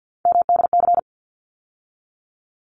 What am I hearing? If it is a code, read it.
Morse code, heard as a keyed tone, about 50 wpm, 700 Hz. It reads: MDC
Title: ARRL Sections 50wpm